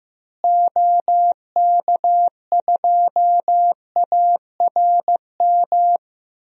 Morse code, heard as a keyed tone, about 15 wpm, 700 Hz. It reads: OK2ARM
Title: DXCC-100 Call Signs - 6 Characters - Repeated 3x - 15wpm